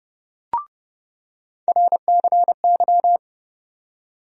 Morse code, heard as a keyed tone, about 32 wpm, 700 Hz. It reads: RCY
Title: USA Call Sign Postfixes - 3 Letters 30wpm